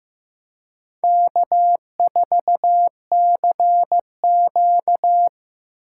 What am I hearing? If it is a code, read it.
Morse code, heard as a keyed tone, about 15 wpm, 700 Hz. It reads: K4CQ
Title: USA Call Signs - 4 Characters - 3x Character Spacing - 15wpm